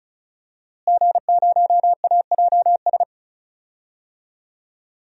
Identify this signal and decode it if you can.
Morse code, heard as a keyed tone, about 36 wpm, 700 Hz. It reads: G0AJS
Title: DXCC-100 Call Signs - 5 Characters - Repeated 3x - 35wpm